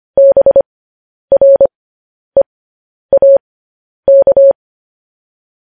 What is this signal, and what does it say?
Morse code, heard as a keyed tone, about 25 wpm, spaced as for 10 wpm, 550 Hz. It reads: BREAK
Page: News Headlines in Morse Code at 10 WPM